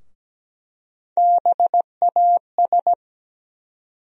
Morse code, heard as a keyed tone, about 17 wpm, 700 Hz. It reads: BAS
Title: USA Call Sign Postfixes - 3 Letters 17wpm